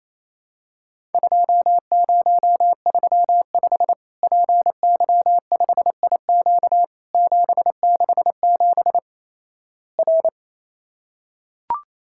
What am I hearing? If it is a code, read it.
Morse code, heard as a keyed tone, about 28 wpm, 700 Hz. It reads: 2035 PY5IQ 767 R
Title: Worked All Europe DX Contest - QTC Exchange - 28wpm